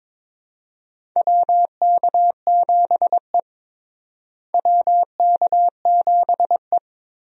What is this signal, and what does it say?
Morse code, heard as a keyed tone, about 21 wpm, 700 Hz. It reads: WK7E WK7E
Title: USA Call Signs - 4 Characters - Repeated 3x - 22wpm